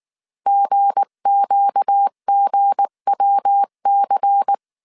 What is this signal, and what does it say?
Morse code, heard as a keyed tone, about 19 wpm, 800 Hz. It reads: GQGWC
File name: CW / DET / 1 9 1500C S B 18wpm 2551